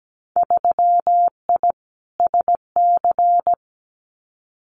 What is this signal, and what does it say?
Morse code, heard as a keyed tone, about 17 wpm, 700 Hz. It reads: 3I SC
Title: Winter Field Day Exchanges 17wpm